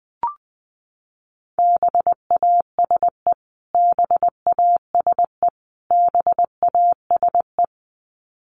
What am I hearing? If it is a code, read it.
Morse code, heard as a keyed tone, about 20 wpm, 700 Hz. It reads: BASE BASE BASE
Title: Top 300 Words Repeated 20wpm